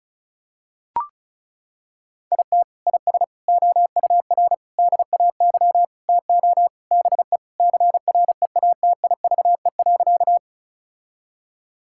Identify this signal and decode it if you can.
Morse code, heard as a keyed tone, about 36 wpm, 700 Hz. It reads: IT IS OUR DAY TO BE CREATIVE.
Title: Sentences from Top 1000 Words - Review Part 5 - 35wpm